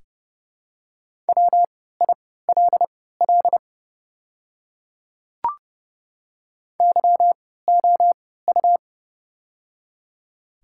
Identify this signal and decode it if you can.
Morse code, heard as a keyed tone, about 31 wpm, spaced as for 17 wpm, 700 Hz. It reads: WILL YOU
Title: 3x Character Spacing - Top 100 Words - 30wpm